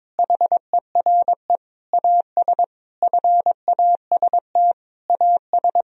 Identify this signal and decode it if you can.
Morse code, heard as a keyed tone, about 22 wpm, 700 Hz. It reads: HERE AS FAST AS
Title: Sentences from Top 1000 Words - Review Part 2 - 22wpm